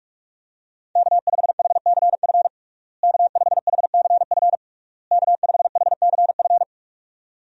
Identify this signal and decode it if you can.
Morse code, heard as a keyed tone, about 46 wpm, 700 Hz. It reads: K5HCF K5HCF K5HCF
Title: USA Call Signs - 5 Characters - Repeated 3x - 45wpm